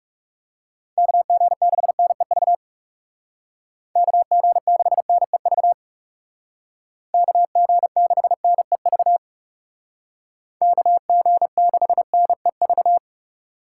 Morse code, heard as a keyed tone, about 37 wpm, 700 Hz. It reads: KG6NEV KG6NEV KG6NEV KG6NEV
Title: Speed Racing - USA 2x3 Call Signs 30wpm